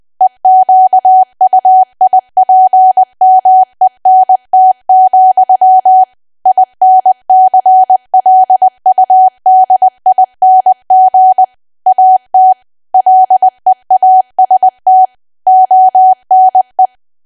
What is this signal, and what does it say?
Morse code, heard as a keyed tone, about 20 wpm, 750 Hz. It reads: EQUIPMENT, INCLUDING AT LEAST ONE
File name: cwa / W1AW-Level3 / 20WPM